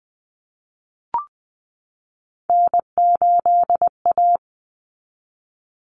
Morse code, heard as a keyed tone, about 20 wpm, 700 Hz. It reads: N8A
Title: Common Cut Numbers - 3-Digit 20wpm